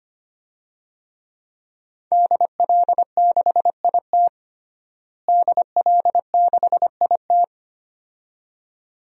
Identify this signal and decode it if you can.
Morse code, heard as a keyed tone, about 25 wpm, 700 Hz. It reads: DL6IT DL6IT
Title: DXCC-100 Call Signs - 5 Characters - Repeated 2x - 25wpm